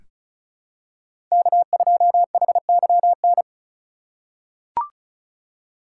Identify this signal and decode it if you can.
Morse code, heard as a keyed tone, about 36 wpm, 700 Hz. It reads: K2HYN
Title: USA Call Signs - 5 Characters - 2x Character Spacing - 35wpm